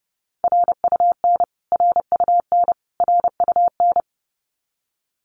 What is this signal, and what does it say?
Morse code, heard as a keyed tone, about 30 wpm, 700 Hz. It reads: RUN RUN RUN
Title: Top 500 Words Repeated 30wpm